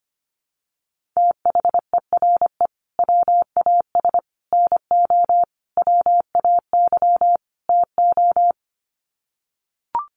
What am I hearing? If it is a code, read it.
Morse code, heard as a keyed tone, about 25 wpm, 700 Hz. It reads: THERE WAS NO WAY TO
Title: Sets of 5 Words Encore 25wpm